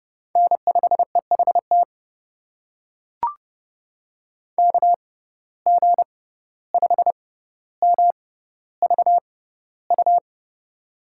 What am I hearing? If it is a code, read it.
Morse code, heard as a keyed tone, about 31 wpm, spaced as for 15 wpm, 700 Hz. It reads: N5EHT KG5MVU
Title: USA Call Signs - 6x Character Spacing - 30wpm